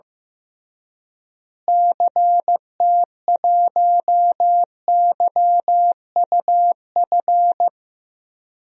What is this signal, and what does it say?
Morse code, heard as a keyed tone, about 15 wpm, 700 Hz. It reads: CT1YUF
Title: DXCC-100 Call Signs - 6 Characters - Repeated 3x - 15wpm